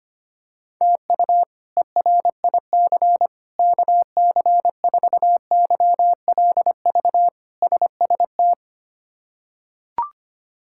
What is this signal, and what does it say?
Morse code, heard as a keyed tone, about 25 wpm, 700 Hz. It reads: TU ERIC KC4YLV SST
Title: K1USN SST QSO Practice - 25wpm